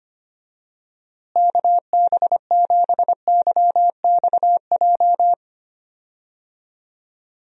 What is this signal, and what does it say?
Morse code, heard as a keyed tone, about 25 wpm, 700 Hz. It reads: KB7YXJ
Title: USA Call Signs - 6 Characters - Repeated 2x - 25wpm